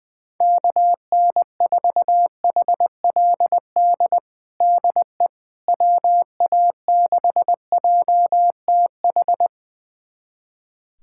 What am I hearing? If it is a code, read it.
Morse code, heard as a keyed tone, about 20 wpm, 700 Hz. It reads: KN4HLD DE WA6JTH
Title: USA Callsign DE Callsign 20wpm